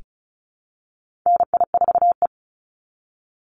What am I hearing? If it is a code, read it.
Morse code, heard as a keyed tone, about 34 wpm, 700 Hz. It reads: NI4E
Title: Call Signs Repeated 2x - 35wpm